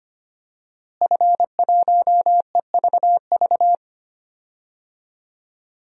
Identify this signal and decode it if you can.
Morse code, heard as a keyed tone, about 25 wpm, 700 Hz. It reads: F1EVV
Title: DXCC-100 Call Signs - 5 Characters - Repeated 2x - 25wpm